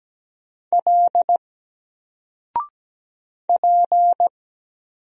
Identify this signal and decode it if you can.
Morse code, heard as a keyed tone, about 17 wpm, 700 Hz. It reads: L P
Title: Single Letter-Number Alt 17wpm